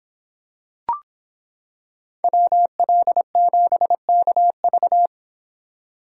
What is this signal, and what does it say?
Morse code, heard as a keyed tone, about 26 wpm, 700 Hz. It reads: WL7KV